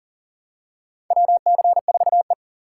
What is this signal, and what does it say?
Morse code, heard as a keyed tone, about 40 wpm, 700 Hz. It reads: WC4E
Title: USA Call Signs - 4 Characters - 2x Character Spacing - 40wpm